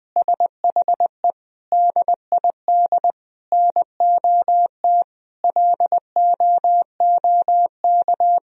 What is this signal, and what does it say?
Morse code, heard as a keyed tone, about 20 wpm, 700 Hz. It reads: SHE DID NOT LOOK
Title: Sentences from Top 1000 Words - Review Part 2 - 20wpm